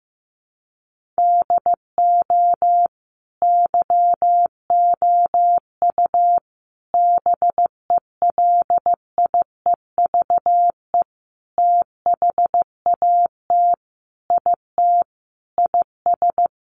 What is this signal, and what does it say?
Morse code, heard as a keyed tone, about 15 wpm, 700 Hz. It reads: DO YOU BELIEVE THAT IT IS
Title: Sentences from Top 1500 Words 15wpm